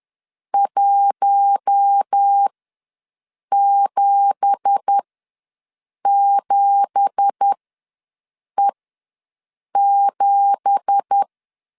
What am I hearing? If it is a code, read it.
Morse code, heard as a keyed tone, about 10 wpm, spaced as for 6 wpm, 800 Hz. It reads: 177E7
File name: LEZIONI / 9 2867 ws 10wpm 4215